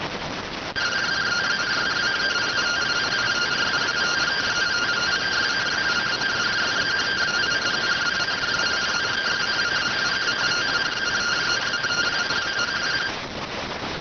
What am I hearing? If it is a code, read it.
Two-tone teleprinter signal, about 170 Hz shift.